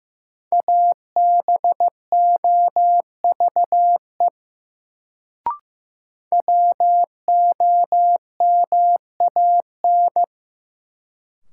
Morse code, heard as a keyed tone, about 15 wpm, 700 Hz. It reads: ABOVE WOMAN
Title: Sentences from Top 1000 Words - Review Part 2 - New Word Drill - 15wpm